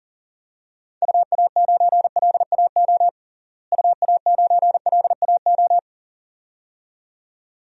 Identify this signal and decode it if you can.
Morse code, heard as a keyed tone, about 40 wpm, 700 Hz. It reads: UA9LAO UA9LAO
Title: DXCC-100 Call Signs - 6 Characters - Repeated 2x - 40wpm